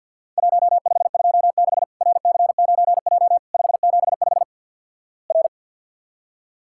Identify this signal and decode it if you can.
Morse code, heard as a keyed tone, about 52 wpm, 700 Hz. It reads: 1526 RC9J 575 R
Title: Worked All Europe DX Contest - QTC Exchange - 50wpm